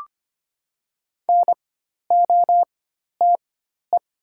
Morse code, heard as a keyed tone, about 25 wpm, spaced as for 12 wpm, 700 Hz. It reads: NOTE
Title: 4x Character Spacing - Top 500 Words - 25wpm